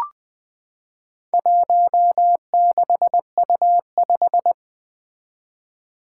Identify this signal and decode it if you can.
Morse code, heard as a keyed tone, about 20 wpm, 700 Hz. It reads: 16U5
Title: Four Challenging Letter-Number Combinations 20wpm